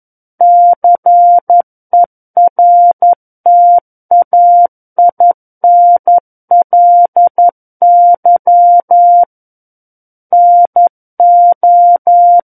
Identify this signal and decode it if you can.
Morse code, heard as a keyed tone, about 11 wpm, 700 Hz. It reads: CERTAINLY NO
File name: Book One / 02-Chapter 2 - 11 WPM